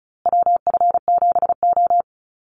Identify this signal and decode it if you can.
Morse code, heard as a keyed tone, about 34 wpm, 700 Hz. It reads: WF7O
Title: USA Call Signs Repeated 2x 35wpm